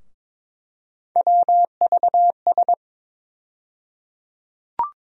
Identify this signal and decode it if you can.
Morse code, heard as a keyed tone, about 22 wpm, 700 Hz. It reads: WVS
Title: USA Call Sign Postfixes - 3 Letters 22wpm